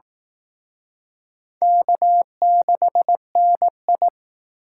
Morse code, heard as a keyed tone, about 19 wpm, 700 Hz. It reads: K6NI